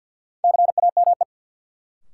Morse code, heard as a keyed tone, about 53 wpm, 700 Hz. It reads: CAME
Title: Sentences from Top 200 Words - New Word Drill - 50wpm